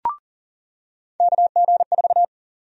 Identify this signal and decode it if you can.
Morse code, heard as a keyed tone, about 40 wpm, 700 Hz. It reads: KG4
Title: USA Call Sign Prefixes - 40wpm